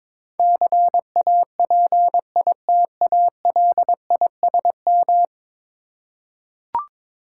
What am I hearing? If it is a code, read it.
Morse code, heard as a keyed tone, about 22 wpm, 700 Hz. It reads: CAPITALISM
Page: morse-code-ninja-practice-22wpm